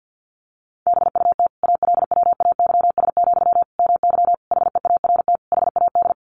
Morse code, heard as a keyed tone, about 50 wpm, 700 Hz. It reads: BUT ALWAYS, MY HEART HAD
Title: Sentences from Top 1200 Words 50wpm